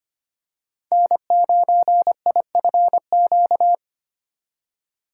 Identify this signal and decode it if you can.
Morse code, heard as a keyed tone, about 25 wpm, 700 Hz. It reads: N9IFQ
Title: USA Call Signs - 5 Characters - 2x Character Spacing - 25wpm